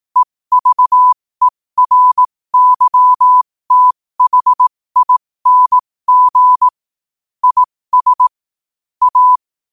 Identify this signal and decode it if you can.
Morse code, heard as a keyed tone, about 18 wpm, spaced as for 15 wpm, 1000 Hz. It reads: EVERYTHING IS A
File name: qotdfiles / 15wpm